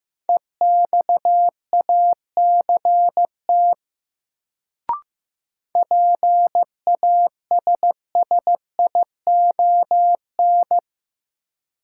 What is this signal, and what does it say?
Morse code, heard as a keyed tone, about 15 wpm, 700 Hz. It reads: EXACT PASSION
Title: Sentences from Top 1300 Words - New Word Drill - 15wpm